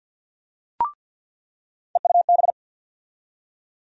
Morse code, heard as a keyed tone, about 47 wpm, 700 Hz. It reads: EUB
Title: USA Call Sign Postfixes - 3 Letters 50wpm